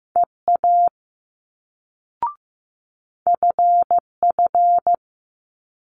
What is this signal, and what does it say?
Morse code, heard as a keyed tone, about 15 wpm, 700 Hz. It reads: EA FF
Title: Multi-letter Phonemes Quick Fire 15wpm